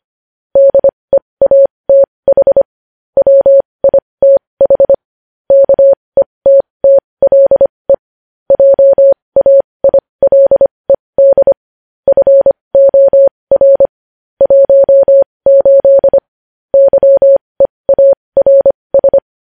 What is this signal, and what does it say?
Morse code, heard as a keyed tone, about 25 wpm, spaced as for 20 wpm, 550 Hz. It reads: DEATH WITH KETTLE JAILED FOR 18 YEARS
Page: news headlines in morse code at 20 wpm